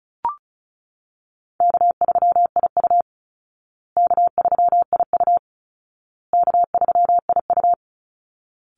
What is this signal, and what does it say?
Morse code, heard as a keyed tone, about 34 wpm, 700 Hz. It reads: K3IU K3IU K3IU
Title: Call Signs Repeated 2x - 35wpm